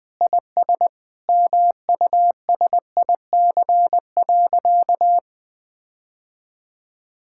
Morse code, heard as a keyed tone, about 20 wpm, 700 Hz. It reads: IS MUSIC.
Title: Sentences with 4-Word Phrases - Spoken Beginning - 20wpm